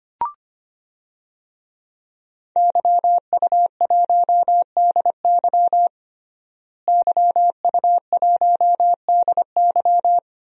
Morse code, heard as a keyed tone, about 25 wpm, 700 Hz. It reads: YU1DY YU1DY
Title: DXCC-100 Call Signs - 5 Characters - Repeated 3x - 25wpm